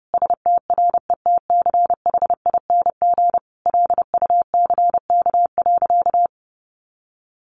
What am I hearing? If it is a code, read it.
Morse code, heard as a keyed tone, about 28 wpm, 700 Hz. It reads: STRETCHING LUCK.
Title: A Sound Of Thunder 30wpm